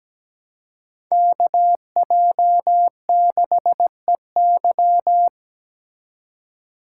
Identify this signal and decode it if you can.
Morse code, heard as a keyed tone, about 20 wpm, 700 Hz. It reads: KJ6EY
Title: Speed Racing - USA Call Signs 17wpm